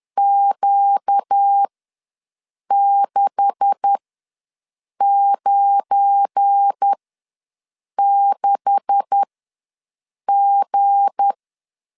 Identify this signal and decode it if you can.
Morse code, heard as a keyed tone, about 10 wpm, spaced as for 6 wpm, 800 Hz. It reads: Q696G
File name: LEZIONI / 9 286 ws 10wpm 4032